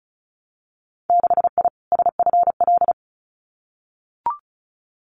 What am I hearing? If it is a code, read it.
Morse code, heard as a keyed tone, about 34 wpm, 700 Hz. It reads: 6I SFL
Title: Winter Field Day Exchanges 35wpm